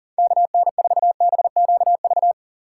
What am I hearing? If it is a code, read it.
Morse code, heard as a keyed tone, about 40 wpm, 700 Hz. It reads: KN4BQV
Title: USA Call Signs - 3x Character Spacing - 40wpm